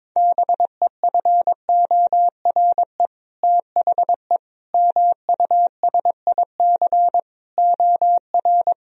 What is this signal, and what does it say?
Morse code, heard as a keyed tone, about 22 wpm, 700 Hz. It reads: BEFORE THE MUSIC OR
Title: Sentences from Top 1000 Words - Review Part 1 - 22wpm